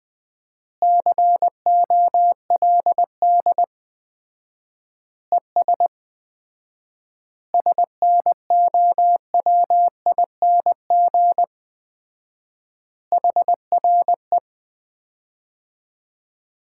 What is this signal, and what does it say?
Morse code, heard as a keyed tone, about 20 wpm, 700 Hz. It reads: COLD ES SNOWING HRE
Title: Common QSO Phrases - 4x Word-Spacing - 20wpm